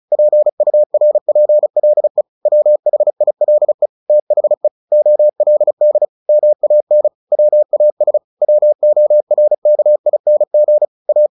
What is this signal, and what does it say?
Morse code, heard as a keyed tone, about 34 wpm, 600 Hz. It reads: PURPLE WHILE THE OLD MAN WAS WORKING A
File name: the old man and the sea / the old man and the sea 1 35wpm